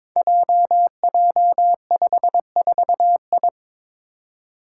Morse code, heard as a keyed tone, about 21 wpm, 700 Hz. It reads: JJ54I
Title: Five Challenging Letter-Number Combinations 22wpm